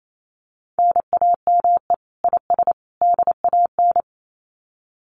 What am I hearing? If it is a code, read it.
Morse code, heard as a keyed tone, about 28 wpm, 700 Hz. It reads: NAME IS DAN
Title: Common QSO Phrases 28wpm